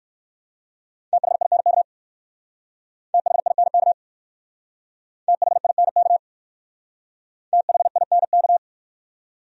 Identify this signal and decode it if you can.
Morse code, heard as a keyed tone, about 55 wpm, 700 Hz. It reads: THINK THINK THINK THINK
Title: Speed-Racing Top 500 Words 45wpm